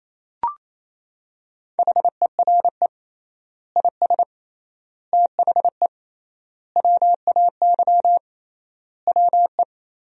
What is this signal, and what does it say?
Morse code, heard as a keyed tone, about 28 wpm, 700 Hz. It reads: HERE IS THE WAY WE
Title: 3x Word-Spacing - Sentences from Top 200 Words - 28wpm